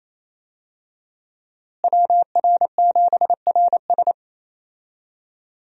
Morse code, heard as a keyed tone, about 28 wpm, 700 Hz. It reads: WR7RS
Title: USA Call Signs - 5 Characters - Repeated 3x - 28wpm